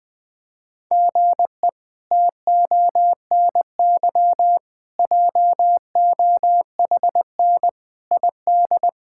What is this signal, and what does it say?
Morse code, heard as a keyed tone, about 20 wpm, 700 Hz. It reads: GE TONY JOHN ID
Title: K1USN SST QSO Practice - 20wpm